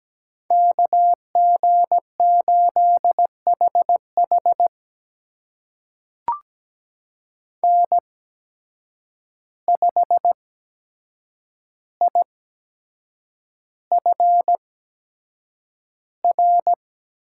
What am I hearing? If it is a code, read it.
Morse code, heard as a keyed tone, about 17 wpm, spaced as for 7 wpm, 700 Hz. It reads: KG8HH N5IFR
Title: USA Call Signs - 5 Characters - 8x Character Spacing - 17wpm